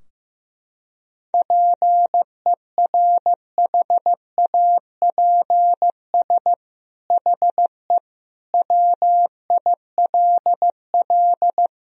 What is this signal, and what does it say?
Morse code, heard as a keyed tone, about 16 wpm, 700 Hz. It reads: PERHAPS HE WILL
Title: Sentences from Top 1000 Words - Review Part 4 - 15wpm